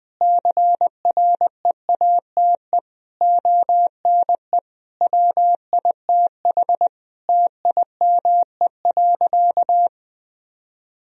Morse code, heard as a keyed tone, about 21 wpm, 700 Hz. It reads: CREATE ONE WITH TIME.
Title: Sentences from Top 1000 Words - Review Part 3 - 20wpm